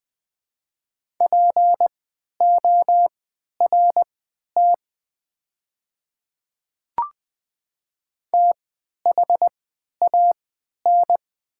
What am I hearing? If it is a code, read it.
Morse code, heard as a keyed tone, about 20 wpm, spaced as for 11 wpm, 700 Hz. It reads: PORT THAN
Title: 3x Character Spacing - Top 300 Words - 20wpm